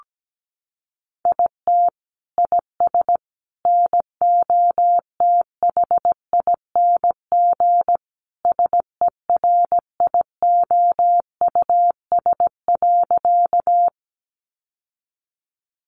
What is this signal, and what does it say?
Morse code, heard as a keyed tone, about 17 wpm, 700 Hz. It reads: IT IS NOTHING SERIOUS.
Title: Sentences from Top 1100 Words 17wpm